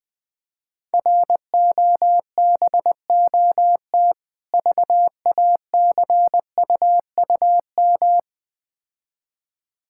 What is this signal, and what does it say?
Morse code, heard as a keyed tone, about 20 wpm, 700 Hz. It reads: ROBOT VACUUM
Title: Difficult Words - Speed-Racing with Hints - 20wpm